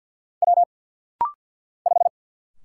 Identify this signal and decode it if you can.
Morse code, heard as a keyed tone, about 50 wpm, 700 Hz. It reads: W 5
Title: Single Letter-Number Alt - Rapid-Fire 50wpm